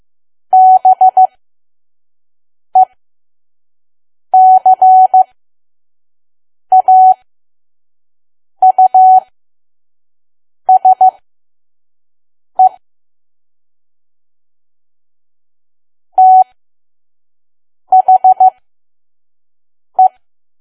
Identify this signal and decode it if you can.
Morse code, heard as a keyed tone, about 15 wpm, spaced as for 5 wpm, 750 Hz. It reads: BECAUSE THE
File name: Archive / 5 WPM files / PM